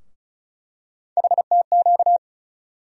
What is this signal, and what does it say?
Morse code, heard as a keyed tone, about 34 wpm, 700 Hz. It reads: HTQ
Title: USA Call Sign Postfixes - 3 Letters 35wpm